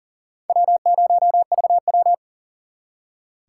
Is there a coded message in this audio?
Yes: Morse code from 0.5 to 2.1 s, heard as a keyed tone, about 50 wpm, 700 Hz. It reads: W0VW